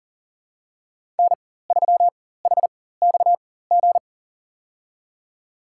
Morse code, heard as a keyed tone, about 40 wpm, spaced as for 19 wpm, 700 Hz. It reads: N3HXG
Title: USA Call Signs - 5 Characters - 4x Character Spacing - 40wpm